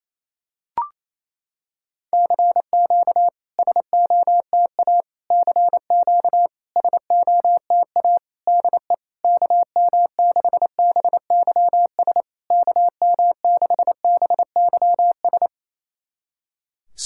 Morse code, heard as a keyed tone, about 28 wpm, 690 Hz. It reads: CQ SOTA CQ SOTA DE KM6BYS KM6BYS
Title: SOTA QSO Practice 28wpm